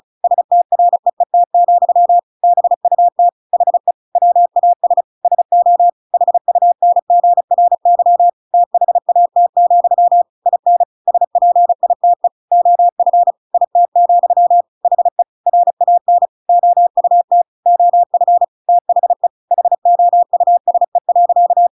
Morse code, heard as a keyed tone, about 36 wpm, 700 Hz. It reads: STREET, BUT HE WAS SO HUNGRY THAT, IN SPITE OF IT, HE RAN OUT OF THE HOUSE.
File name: Pinocchio / Pinocchio ch6 35 35 ik1hge